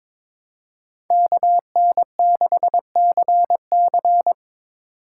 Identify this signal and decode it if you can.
Morse code, heard as a keyed tone, about 22 wpm, 700 Hz. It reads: KN6CC
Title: USA Call Signs - 5 Characters - 2x Character Spacing - 22wpm